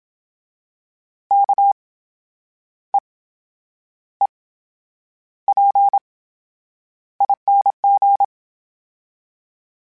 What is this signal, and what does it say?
Morse code, heard as a keyed tone, about 27 wpm, spaced as for 9 wpm, 800 Hz. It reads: KEEPING
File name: Special INGs / Special INGs-mp3s / VerySlowToFaster Ing @25